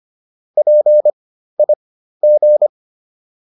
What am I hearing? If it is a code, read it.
Morse code, heard as a keyed tone, about 25 wpm, spaced as for 13 wpm, 600 Hz. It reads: PIG